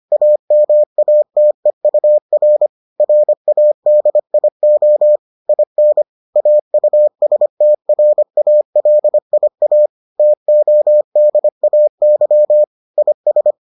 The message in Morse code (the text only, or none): AMATEUR RADIO IN AUSTRALIA TODAY IS